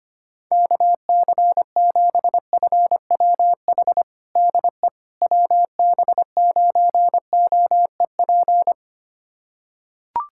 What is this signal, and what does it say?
Morse code, heard as a keyed tone, about 25 wpm, 700 Hz. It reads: KC7FWH DE WB9OEP
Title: USA Callsign DE Callsign - 6 Characters - 25wpm